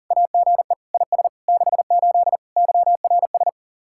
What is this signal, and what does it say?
Morse code, heard as a keyed tone, about 52 wpm, 700 Hz. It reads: AGE IS 68 YRS